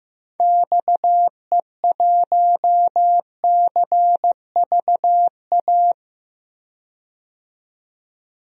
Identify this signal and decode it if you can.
Morse code, heard as a keyed tone, about 15 wpm, 700 Hz. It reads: XE1CVA
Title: DXCC-100 Call Signs - 6 Characters - 15wpm